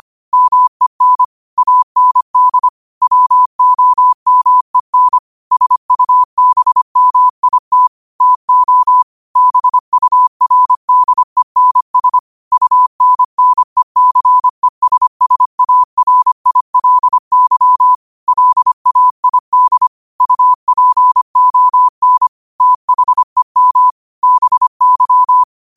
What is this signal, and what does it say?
Morse code, heard as a keyed tone, about 25 wpm, 1000 Hz. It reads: MEN AND WOMEN SUBMIT TO BURDENS UNNECESSARILY LAID UPON THEM BY
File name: qotdfiles / 25wpm